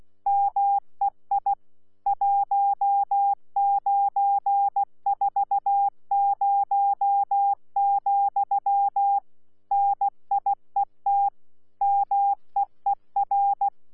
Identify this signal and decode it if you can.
Morse code, heard as a keyed tone, about 16 wpm, 800 Hz. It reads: MEI 1940, NIET MEER
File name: Z PI4VRZ / PI4VRZ 14-jun-2025